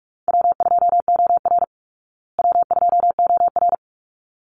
Morse code, heard as a keyed tone, about 44 wpm, 700 Hz. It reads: W2OR W2OR
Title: Easy Call Signs Repeated 2x - 45wpm